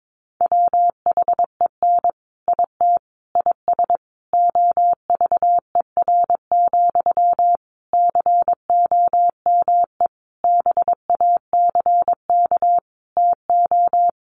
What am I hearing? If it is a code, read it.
Morse code, heard as a keyed tone, about 22 wpm, 700 Hz. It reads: WHEN IT IS OVER, COME BACK TO
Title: Sentences from Top 1200 Words 22wpm